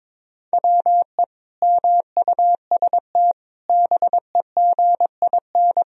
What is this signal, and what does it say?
Morse code, heard as a keyed tone, about 22 wpm, 700 Hz. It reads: WE MUST BEGIN
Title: Sentences from Top 1000 Words - Review Part 1 - 22wpm